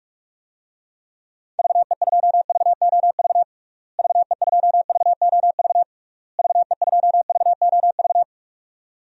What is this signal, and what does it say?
Morse code, heard as a keyed tone, about 46 wpm, 700 Hz. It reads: VE2VOV VE2VOV VE2VOV
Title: DXCC-100 Call Signs - 6 Characters - Repeated 3x - 45wpm